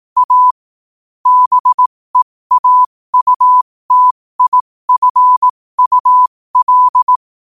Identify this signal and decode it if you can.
Morse code, heard as a keyed tone, about 18 wpm, spaced as for 15 wpm, 1000 Hz. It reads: A BEAUTIFUL